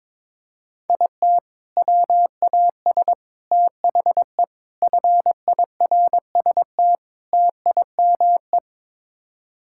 Morse code, heard as a keyed tone, about 22 wpm, 700 Hz. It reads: IT WAS THE FIRST TIME
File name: practice / 22wpm / Build A Phrase - 5-Word Phrases - 22wpm